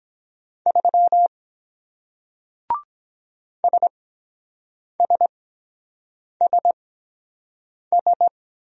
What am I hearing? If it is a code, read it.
Morse code, heard as a keyed tone, about 23 wpm, spaced as for 7 wpm, 700 Hz. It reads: 3 SSSS